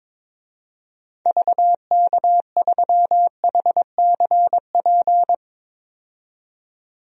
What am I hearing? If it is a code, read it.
Morse code, heard as a keyed tone, about 22 wpm, 700 Hz. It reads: VK3HCP
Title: DXCC-100 Call Signs - 6 Characters - 22wpm